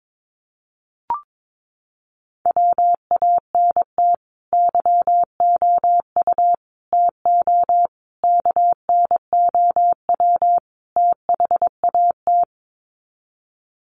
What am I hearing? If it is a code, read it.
Morse code, heard as a keyed tone, about 22 wpm, 700 Hz. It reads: WANT YOU TO KNOW THAT
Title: Sets of 5 Words Encore 22wpm